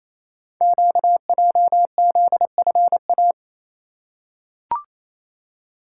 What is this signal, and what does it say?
Morse code, heard as a keyed tone, about 29 wpm, 700 Hz. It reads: QJZFA
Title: Code Groups - 5 Letters - 28wpm